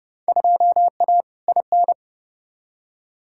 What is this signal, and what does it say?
Morse code, heard as a keyed tone, about 30 wpm, 700 Hz. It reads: 2A IN
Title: Field Day Exchanges - 30wpm